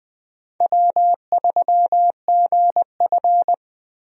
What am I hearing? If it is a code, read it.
Morse code, heard as a keyed tone, about 20 wpm, 700 Hz. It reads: W3GF
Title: USA Call Signs - 4 Characters - 20wpm